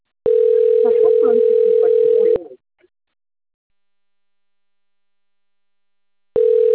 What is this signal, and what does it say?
Ringback tone (North America), 440 + 480 Hz, with speech.
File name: rg-1600-unknown- - - .67399